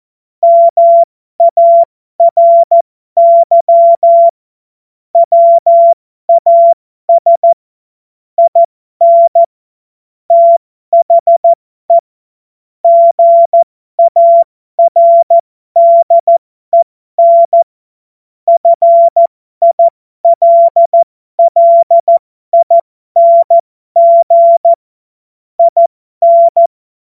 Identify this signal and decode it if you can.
Morse code, heard as a keyed tone, about 14 wpm, spaced as for 12 wpm, 680 Hz. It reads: MARY WAS IN THE GARDEN FILLING IN
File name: Morsekurs / Witze